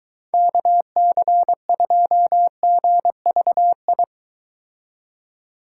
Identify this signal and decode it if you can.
Morse code, heard as a keyed tone, about 23 wpm, 700 Hz. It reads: KC2GVI